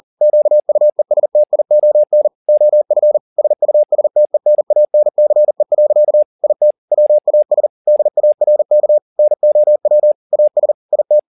Morse code, heard as a keyed tone, about 40 wpm, 600 Hz. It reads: QUESTION OF SUSTENANCE. IT WAS DARK NOW AS IT
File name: the old man and the sea 2 40wpm